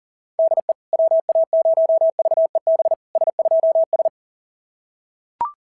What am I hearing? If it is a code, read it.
Morse code, heard as a keyed tone, about 40 wpm, 690 Hz. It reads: DE WA0VEB S2S
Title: SOTA QSO Practice 40wpm